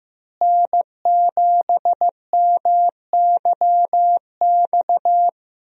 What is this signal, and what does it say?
Morse code, heard as a keyed tone, about 15 wpm, 700 Hz. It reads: N7MYX
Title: USA Call Signs - 5 Characters - Repeated 2x - 15wpm